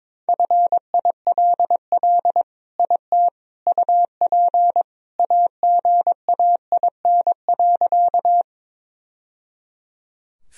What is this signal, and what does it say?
Morse code, heard as a keyed tone, about 22 wpm, 700 Hz. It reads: FILL IT UP AGAIN.
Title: Sentences from Top 1000 Words - Review Part 3 - 22wpm